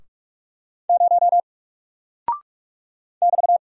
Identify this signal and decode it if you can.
Morse code, heard as a keyed tone, about 44 wpm, 700 Hz. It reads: 0 <BT>
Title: Single Character (Letter:Number:Punctuation:Prosign) Alt 45wpm